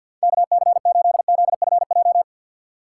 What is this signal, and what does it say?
Morse code, heard as a keyed tone, about 47 wpm, 700 Hz. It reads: KC8ZFJ
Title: USA Call Signs - 6 Characters - 6x Character Spacing - 50wpm